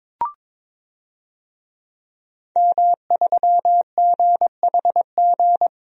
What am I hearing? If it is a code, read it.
Morse code, heard as a keyed tone, about 22 wpm, 700 Hz. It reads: M3GHG
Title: DXCC-100 Call Signs - 5 Characters - Repeated 2x - 22wpm